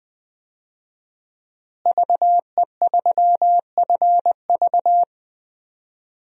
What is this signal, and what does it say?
Morse code, heard as a keyed tone, about 21 wpm, 700 Hz. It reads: VE3FV
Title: DXCC-100 Call Signs - 5 Characters - Repeated 3x - 20wpm